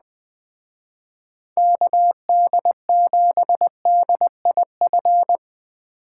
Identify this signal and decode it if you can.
Morse code, heard as a keyed tone, about 20 wpm, 700 Hz. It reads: KD7DIF
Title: Speed Racing - USA 2x3 Call Signs 17wpm